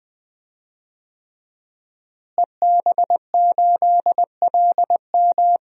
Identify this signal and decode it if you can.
Morse code, heard as a keyed tone, about 20 wpm, 700 Hz. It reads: EB8LM
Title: DXCC-100 Call Signs - 5 Characters - Repeated 3x - 20wpm